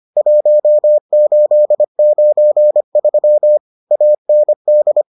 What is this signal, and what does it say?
Morse code, heard as a keyed tone, about 25 wpm, 600 Hz. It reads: 1893 AND